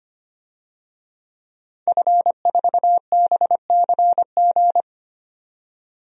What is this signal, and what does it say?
Morse code, heard as a keyed tone, about 25 wpm, 700 Hz. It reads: F4BCG
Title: DXCC-100 Call Signs - 5 Characters - 25wpm